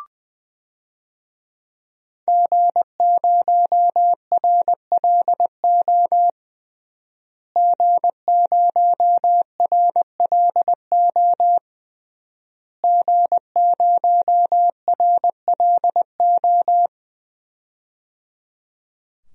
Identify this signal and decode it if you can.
Morse code, heard as a keyed tone, about 20 wpm, 700 Hz. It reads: G0RLO G0RLO G0RLO
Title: DXCC-100 Call Signs - 5 Characters - Repeated 3x - 20wpm